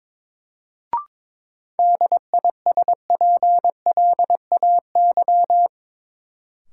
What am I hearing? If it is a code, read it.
Morse code, heard as a keyed tone, about 22 wpm, 700 Hz. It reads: DISPLAY
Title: Sentences from Top 1200 Words - New Word Drill - 22wpm